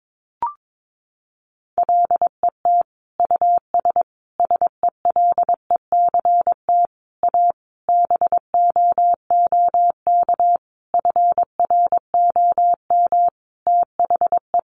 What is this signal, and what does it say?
Morse code, heard as a keyed tone, about 22 wpm, 700 Hz. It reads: LET US SELECT A BOOK FROM THE
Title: Sentences from Top 1300 Words 22wpm